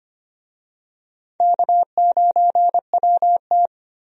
Morse code, heard as a keyed tone, about 25 wpm, 700 Hz. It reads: K9WT
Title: USA Call Signs - 4 Characters - 25wpm